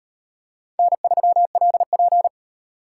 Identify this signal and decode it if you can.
Morse code, heard as a keyed tone, about 37 wpm, 700 Hz. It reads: N3LP